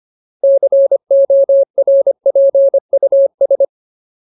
Morse code, heard as a keyed tone, about 25 wpm, 550 Hz. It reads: CORPUS
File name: Ultimate Word List / 25WPM Ultimate